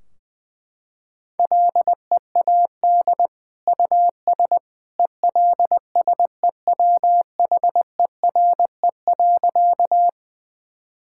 Morse code, heard as a keyed tone, about 21 wpm, 700 Hz. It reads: LEAD US ELSEWHERE.
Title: Sentences from Top 1600 Words 20wpm